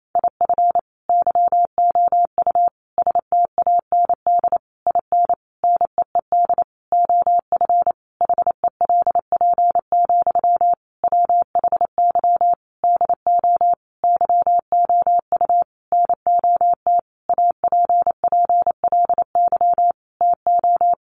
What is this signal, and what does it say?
Morse code, heard as a keyed tone, about 28 wpm, 700 Hz. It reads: IF YOU STAND IN NEED OF HELP, WHY DO YOU NOT APPLY TO
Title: Sentences from Top 1200 Words 28wpm